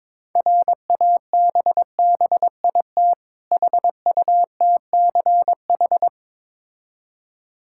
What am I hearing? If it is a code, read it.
Morse code, heard as a keyed tone, about 22 wpm, 690 Hz. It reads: RABBIT HUTCH
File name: Difficult Words with Hints - 22wpm